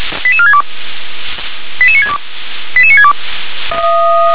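Five-tone selective call (ZVEI), heard as quick runs of five tones, tones of about 70 ms. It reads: 88462 88962 88962
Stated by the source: ZVEI five-tone selective call 88775, received with an SDR.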